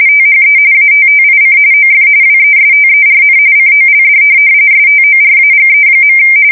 RTTY, about 45.45 baud. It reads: THE ATTIC OF WESTERN COLD STORAGE.
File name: fsk170 / teletype-corp